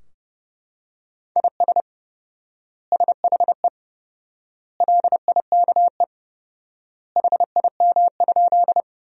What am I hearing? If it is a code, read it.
Morse code, heard as a keyed tone, about 32 wpm, 700 Hz. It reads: IS SHE LIKE HIM?
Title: 4x Word-Spacing - Sentences from Top 100 Words - 30wpm